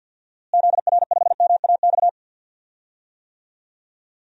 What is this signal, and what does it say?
Morse code, heard as a keyed tone, about 46 wpm, 700 Hz. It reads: ZR5MAX
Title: DXCC-100 Call Signs - 6 Characters - Repeated 3x - 50wpm